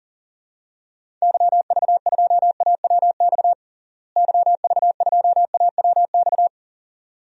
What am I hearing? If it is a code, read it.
Morse code, heard as a keyed tone, about 40 wpm, 700 Hz. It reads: YV2AWX YV2AWX
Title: DXCC-100 Call Signs - 6 Characters - Repeated 2x - 40wpm